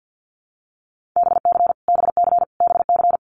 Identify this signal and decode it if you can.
Morse code, heard as a keyed tone, about 49 wpm, 700 Hz. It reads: BC BC BC
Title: Canadian Provinces and Territories - Repeated 3x - 50wpm